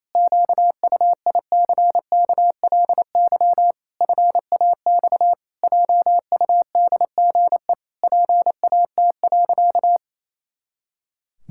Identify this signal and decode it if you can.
Morse code, heard as a keyed tone, about 28 wpm, 700 Hz. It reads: QUICKLY FAX JUDGE PAT.
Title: Pangram Challenge v2 - 28wpm